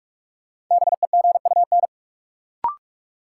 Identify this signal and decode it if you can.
Morse code, heard as a keyed tone, about 43 wpm, 700 Hz. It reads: BEGUN
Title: Sentences from Top 1000 Words - Review Part 6 - New Word Drill - 45wpm